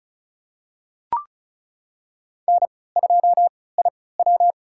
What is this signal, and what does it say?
Morse code, heard as a keyed tone, about 34 wpm, spaced as for 20 wpm, 700 Hz. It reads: N2IW
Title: USA Call Signs - 4 Characters - 3x Character Spacing - 35wpm